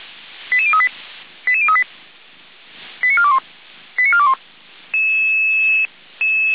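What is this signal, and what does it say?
Five-tone selective call (ZVEI), heard as quick runs of five tones, tones of about 70 ms. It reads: 88038 88038 89421 89421
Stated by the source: ZVEI five-tone selective call received on SDR.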